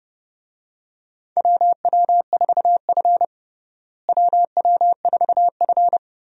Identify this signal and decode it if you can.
Morse code, heard as a keyed tone, about 32 wpm, 700 Hz. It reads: WW4F WW4F
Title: USA Call Signs - 4 Characters - Repeated 3x - 30wpm